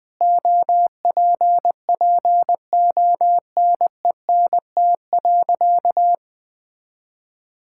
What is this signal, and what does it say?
Morse code, heard as a keyed tone, about 20 wpm, 700 Hz. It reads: OPPONENT.
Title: Pangram Challenge v2 - 20wpm